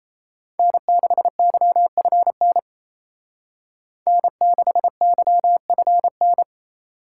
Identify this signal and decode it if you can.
Morse code, heard as a keyed tone, about 33 wpm, 700 Hz. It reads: N6YFN N6YFN
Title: Speed Racing - USA 1x3 Call Signs 28wpm